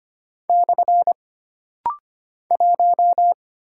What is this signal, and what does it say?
Morse code, heard as a keyed tone, about 25 wpm, 700 Hz. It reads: / 1
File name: Single Character (Letter:Number:Punctuation:Prosign) - Rapid-Fire - 25wpm